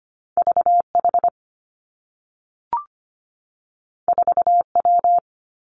Morse code, heard as a keyed tone, about 25 wpm, 700 Hz. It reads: VH 4W
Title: Two Challenging Letter-Number Combinations 25wpm